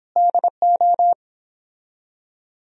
Morse code, heard as a keyed tone, about 29 wpm, 700 Hz. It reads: DO